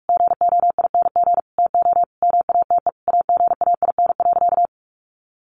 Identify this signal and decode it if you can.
Morse code, heard as a keyed tone, about 44 wpm, 700 Hz. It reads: GOING TO MATE AGAIN.
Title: A Sound Of Thunder 45wpm